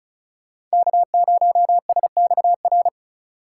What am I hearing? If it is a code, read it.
Morse code, heard as a keyed tone, about 36 wpm, 690 Hz. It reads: K0SXR
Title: USA Call Signs - 5 Characters - 3x Character Spacing - 35wpm